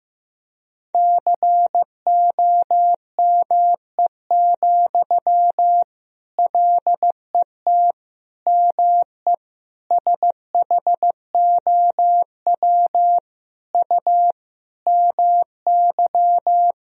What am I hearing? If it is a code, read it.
Morse code, heard as a keyed tone, about 15 wpm, 700 Hz. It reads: COME, LET ME SHOW U MY
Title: Sentences from Top 500 Words and Abbreviations 15wpm